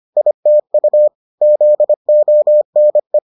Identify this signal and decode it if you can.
Morse code, heard as a keyed tone, about 25 wpm, 600 Hz. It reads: ITU ZONE